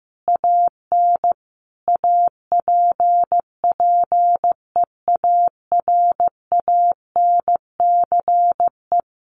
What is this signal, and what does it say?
Morse code, heard as a keyed tone, about 15 wpm, 700 Hz. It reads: AN APPEARANCE
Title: Sentences from Top 1100 Words 15wpm